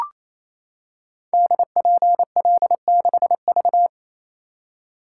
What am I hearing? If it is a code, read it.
Morse code, heard as a keyed tone, about 28 wpm, 700 Hz. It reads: DPL6V
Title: Code Groups - 5 Letter:Numbers - 28wpm